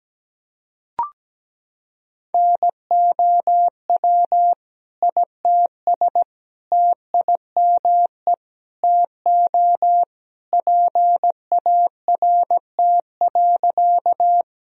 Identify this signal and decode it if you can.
Morse code, heard as a keyed tone, about 17 wpm, 700 Hz. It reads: NOW ITS TIME TO PART.
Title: Sentences from Top 1000 Words - Review Part 1 - 17wpm